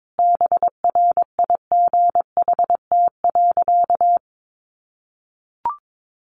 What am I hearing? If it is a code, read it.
Morse code, heard as a keyed tone, about 22 wpm, 700 Hz. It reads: BRIGHT.
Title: Sentences from Top 1500 Words 22wpm